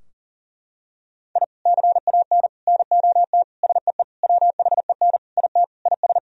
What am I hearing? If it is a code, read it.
Morse code, heard as a keyed tone, about 37 wpm, 700 Hz. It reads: I CAN NOT SEE WHEN IT IS
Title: Sentences from Top 1000 Words - Review Part 4 - 40wpm